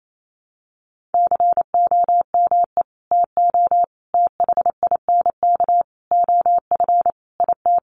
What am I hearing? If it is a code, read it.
Morse code, heard as a keyed tone, about 28 wpm, 700 Hz. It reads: COME TO THINK OF IT
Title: Sets of 5 Words Encore 28wpm